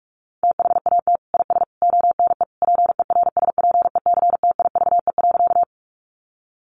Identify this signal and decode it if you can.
Morse code, heard as a keyed tone, about 46 wpm, 700 Hz. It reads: THAT IS ONE PERSPECTIVE.
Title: Sentences from Top 1200 Words 45wpm